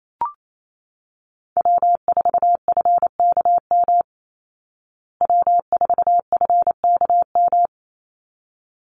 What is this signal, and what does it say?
Morse code, heard as a keyed tone, about 28 wpm, 700 Hz. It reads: W4FKM W4FKM
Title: Call Signs Repeated 3x - 28wpm